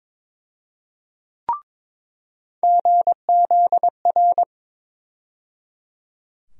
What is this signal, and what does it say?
Morse code, heard as a keyed tone, about 22 wpm, 700 Hz. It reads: GZR
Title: USA Call Sign Postfixes - 3 Letters 22wpm